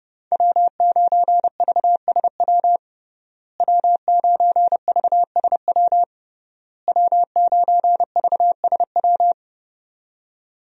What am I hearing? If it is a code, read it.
Morse code, heard as a keyed tone, about 30 wpm, 700 Hz. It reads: W9VSW W9VSW W9VSW
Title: USA Call Signs - 5 Characters - Repeated 3x - 30wpm